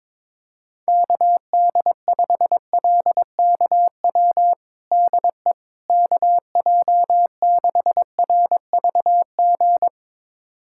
Morse code, heard as a keyed tone, about 22 wpm, 700 Hz. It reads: KD5LKW DE KJ6RVG
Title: USA Callsign DE Callsign - 6 Characters - 22wpm